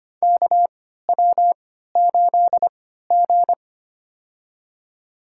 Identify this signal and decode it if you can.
Morse code, heard as a keyed tone, about 25 wpm, spaced as for 14 wpm, 700 Hz. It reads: KW8G
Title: USA Call Signs - 4 Characters - 3x Character Spacing - 25wpm